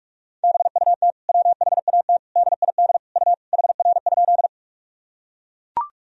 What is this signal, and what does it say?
Morse code, heard as a keyed tone, about 43 wpm, 700 Hz. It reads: BUT WHAT DID U HR?
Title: Sentences from Top 300 Words and Abbreviations 45wpm